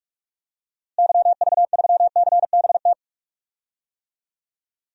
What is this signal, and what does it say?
Morse code, heard as a keyed tone, about 46 wpm, 700 Hz. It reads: YV3CBT
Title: DXCC-100 Call Signs - 6 Characters - Repeated 3x - 45wpm